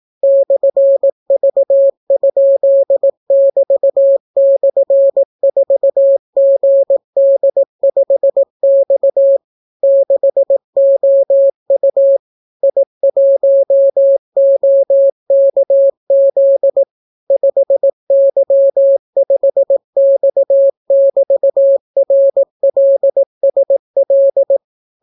Morse code, heard as a keyed tone, about 18 wpm, 550 Hz. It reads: /V?<BT>/4GD5X 6OU I1OKZ 5Y5X<BT>RLSL